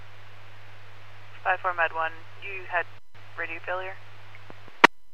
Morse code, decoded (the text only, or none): none